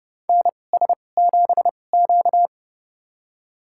Morse code, heard as a keyed tone, about 30 wpm, spaced as for 22 wpm, 700 Hz. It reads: NS7Q
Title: USA Call Signs - 4 Characters - 2x Character Spacing - 30wpm